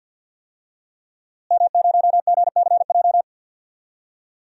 Morse code, heard as a keyed tone, about 51 wpm, 700 Hz. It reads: M0GCJ
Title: DXCC-100 Call Signs - 5 Characters - Repeated 2x - 50wpm